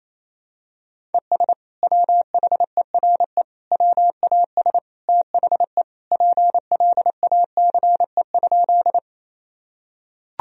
Morse code, heard as a keyed tone, about 28 wpm, 700 Hz. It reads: ES WHERE WAS THE PLACE?
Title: Sentences from Top 200 Words and Abbreviations 28wpm